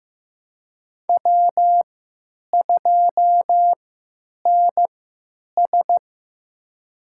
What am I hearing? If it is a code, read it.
Morse code, heard as a keyed tone, about 15 wpm, spaced as for 9 wpm, 700 Hz. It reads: W2NS
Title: USA Call Signs - 4 Characters - 3x Character Spacing - 15wpm